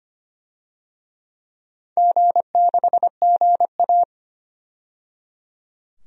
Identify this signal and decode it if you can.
Morse code, heard as a keyed tone, about 25 wpm, 700 Hz. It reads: G6GA
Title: DXCC-100 Call Signs - 4 Characters - 25wpm